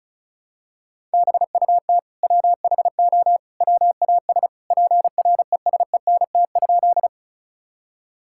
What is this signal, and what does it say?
Morse code, heard as a keyed tone, about 34 wpm, 700 Hz. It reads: BUT WHO WAS PRESENT?
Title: Sentences from Top 1000 Words - Review Part 5 - 35wpm